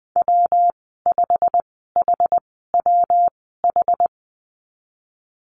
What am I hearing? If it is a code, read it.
Morse code, heard as a keyed tone, about 19 wpm, spaced as for 14 wpm, 700 Hz. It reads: W5HWH
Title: 147 Lesson 24g - Letter y - Call Signs 20wpm